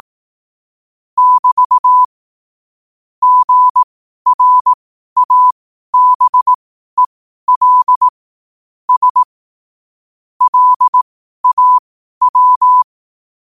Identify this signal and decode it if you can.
Morse code, heard as a keyed tone, about 18 wpm, spaced as for 12 wpm, 1000 Hz. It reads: <BT> GRABEL S LAW
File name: qotdfiles / 12wpm